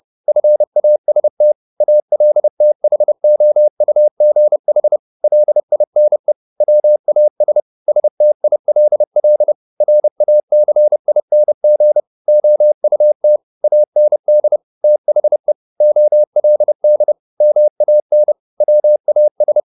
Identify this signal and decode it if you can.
Morse code, heard as a keyed tone, about 30 wpm, 600 Hz. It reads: FAST ALTHOUGH LINE WAS STILL RACING OUT AND THE OLD MAN WAS